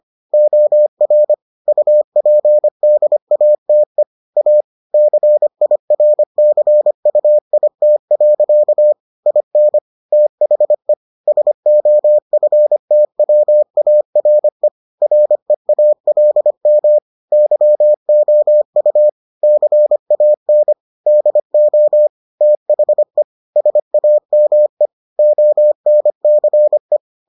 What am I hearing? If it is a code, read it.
Morse code, heard as a keyed tone, about 25 wpm, 600 Hz. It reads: OR UPDATE A CIRCUIT. IN THE SOFTWARE REALM YOU CAN DO THE SAME ONCE